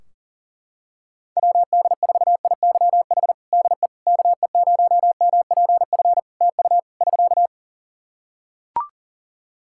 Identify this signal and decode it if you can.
Morse code, heard as a keyed tone, about 40 wpm, 700 Hz. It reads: WD4IYH DE KE0MPF TU <SK>
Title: Practice QSO - Exchange 4 - Ending - 40wpm